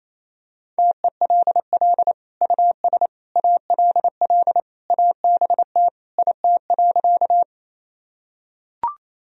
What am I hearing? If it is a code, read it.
Morse code, heard as a keyed tone, about 28 wpm, 700 Hz. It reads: TELL US ALL ABT IT.
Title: Sentences from Top 200 Words and Abbreviations 28wpm